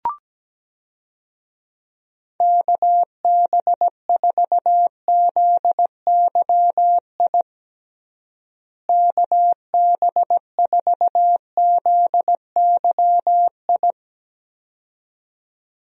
Morse code, heard as a keyed tone, about 17 wpm, 700 Hz. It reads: KB4ZYI KB4ZYI
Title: USA Call Signs - 6 Characters - Repeated 2x - 17wpm